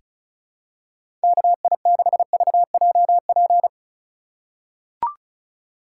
Morse code, heard as a keyed tone, about 36 wpm, 700 Hz. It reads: KI6VJP
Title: USA Call Signs - 6 Characters - 4x Character Spacing - 35wpm